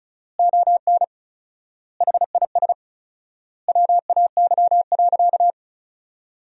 Morse code, heard as a keyed tone, about 34 wpm, 700 Hz. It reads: ON HIS WAY.
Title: 4x Word-Spacing - Sentences from Top 300 Words - 35wpm